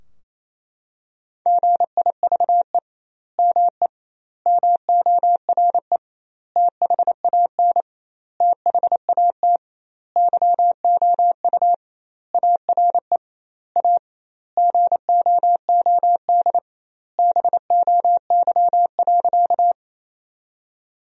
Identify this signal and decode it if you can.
Morse code, heard as a keyed tone, about 28 wpm, 700 Hz. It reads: GIVE ME MORE THAN THAT YOU ARE A GOOD BOY.
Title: 2x Word-Spacing - Sentences from Top 200 Words - 28wpm